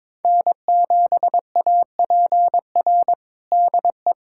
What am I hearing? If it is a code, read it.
Morse code, heard as a keyed tone, about 22 wpm, 700 Hz. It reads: N7APR DE
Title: USA Callsign DE Callsign - 5 Characters - 22wpm